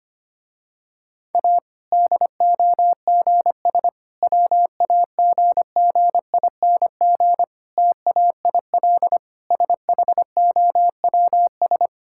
Morse code, heard as a keyed tone, about 25 wpm, 700 Hz. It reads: A DOGS WAGGING TAIL SHOWS
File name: practice / 25wpm / Difficult Sentences with Hints - 25wpm